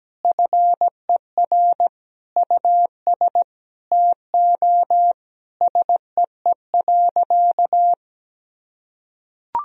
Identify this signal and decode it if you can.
Morse code, heard as a keyed tone, about 17 wpm, 700 Hz. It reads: FER US TO SEE.
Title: Sentences from Top 500 Words and Abbreviations 17wpm